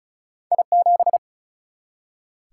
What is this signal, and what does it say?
Morse code, heard as a keyed tone, about 36 wpm, 700 Hz. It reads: I 7
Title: Single-Letter Single-Number 35wpm